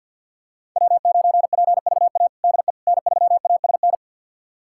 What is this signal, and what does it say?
Morse code, heard as a keyed tone, about 52 wpm, 690 Hz. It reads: W9PVA DE N3ASN
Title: USA Callsign DE Callsign - 5 Characters - 50wpm